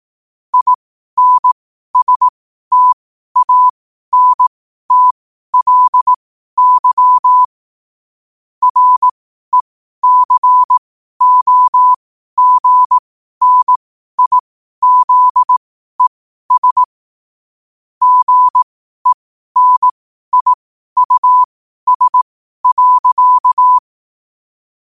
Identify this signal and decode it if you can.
Morse code, heard as a keyed tone, about 18 wpm, spaced as for 12 wpm, 1000 Hz. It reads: INSTANTLY RECOGNIZES GENIUS.
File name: qotdfiles / 12wpm